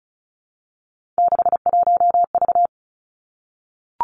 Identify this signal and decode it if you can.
Morse code, heard as a keyed tone, about 36 wpm, 700 Hz. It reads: 61V
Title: Three Challenging Letter-Number Combinations 35wpm